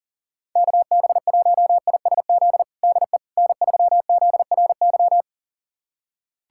Morse code, heard as a keyed tone, about 40 wpm, 700 Hz. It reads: KB1ISZ DE N3ZRY
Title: USA Callsign DE Callsign 40wpm